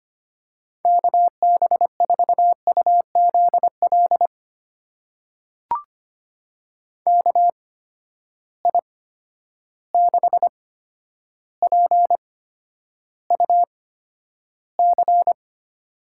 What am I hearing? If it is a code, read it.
Morse code, heard as a keyed tone, about 25 wpm, spaced as for 11 wpm, 700 Hz. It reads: KB4UZL KI6PUC
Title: USA Call Signs - 6 Characters - 8x Character Spacing - 25wpm